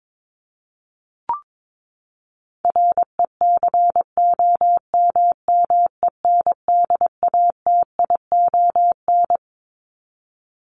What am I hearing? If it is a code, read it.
Morse code, heard as a keyed tone, about 22 wpm, 700 Hz. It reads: RECOMMENDATION
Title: Top 1251-1500 Words 22wpm